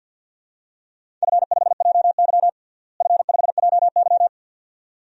Morse code, heard as a keyed tone, about 50 wpm, 700 Hz. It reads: F5JY F5JY
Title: DXCC-100 Call Signs - 4 Characters - Repeated 2x - 50wpm